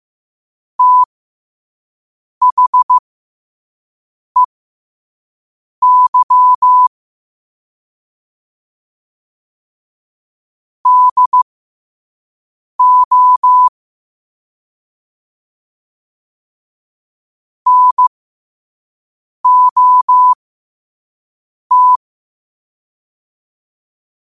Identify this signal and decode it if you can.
Morse code, heard as a keyed tone, about 15 wpm, spaced as for 5 wpm, 1000 Hz. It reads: THEY DO NOT
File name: qotdfiles / 5wpm